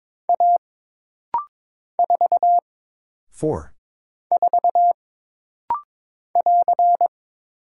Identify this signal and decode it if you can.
Morse code, heard as a keyed tone, about 22 wpm, 700 Hz. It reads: A 4 4 <AR>
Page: This rapid-fire timing is geared to push your ICR (Instant Character Recognition) skills.